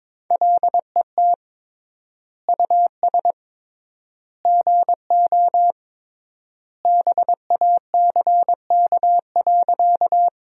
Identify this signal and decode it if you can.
Morse code, heard as a keyed tone, about 22 wpm, 700 Hz. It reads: LET US GO BACK.
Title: 3x Word-Spacing - Sentences from Top 300 Words - 22wpm